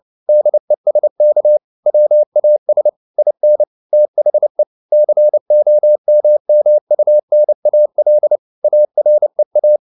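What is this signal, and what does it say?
Morse code, heard as a keyed tone, about 29 wpm, 600 Hz. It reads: DESK WAS IN THE COMMUNAL AREA
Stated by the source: Morse code at 29 wpm, sending Ken Follett's novel Never in English, from CW med Gnister.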